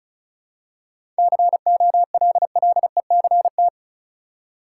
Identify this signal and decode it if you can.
Morse code, heard as a keyed tone, about 34 wpm, 700 Hz. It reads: COLLECT
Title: Seven Letter Words - 35wpm